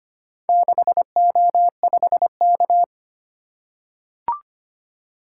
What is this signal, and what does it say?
Morse code, heard as a keyed tone, about 25 wpm, 700 Hz. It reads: 6O5K
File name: practice / 25wpm / Four Letter-Number Combinations - 25wpm